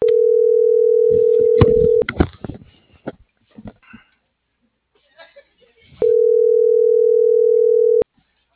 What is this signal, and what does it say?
Ringback tone (North America), 440 + 480 Hz.
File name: CENTRAL MAERSK / out adolfo.msk - .72573